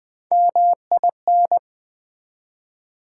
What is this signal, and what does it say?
Morse code, heard as a keyed tone, about 20 wpm, 700 Hz. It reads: MIN
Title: Build A Word - 1 Letter At A Time - 3 to 5-Letter Words - 20wpm